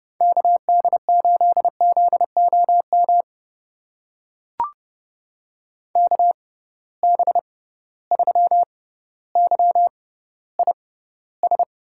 Morse code, heard as a keyed tone, about 32 wpm, spaced as for 16 wpm, 700 Hz. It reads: KD8ZOM KB3YIS
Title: USA Call Signs - 6 Characters - 6x Character Spacing - 30wpm